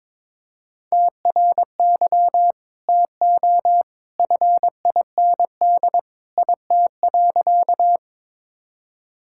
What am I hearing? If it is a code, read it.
Morse code, heard as a keyed tone, about 22 wpm, 700 Hz. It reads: TRY TO FIND IT.
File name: practice / 22wpm / Sentences with 2-Word Phrases - 22wpm